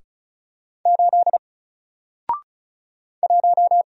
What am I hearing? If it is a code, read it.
Morse code, heard as a keyed tone, about 36 wpm, 700 Hz. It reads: 8 1
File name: Single-Letter-Number-35wpm